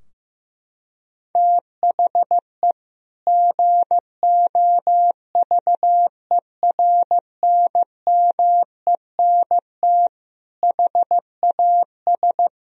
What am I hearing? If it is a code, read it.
Morse code, heard as a keyed tone, about 16 wpm, 700 Hz. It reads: THE GOVERNMENT HAS
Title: Sentences from Top 1000 Words - Review Part 3 - 15wpm